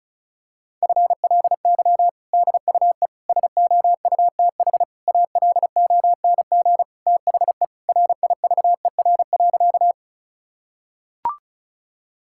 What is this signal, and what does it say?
Morse code, heard as a keyed tone, about 36 wpm, 700 Hz. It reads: FLY DUE SOUTH ALONG THE RIVER.
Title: Sentences from Top 1000 Words - Review Part 6 - 35wpm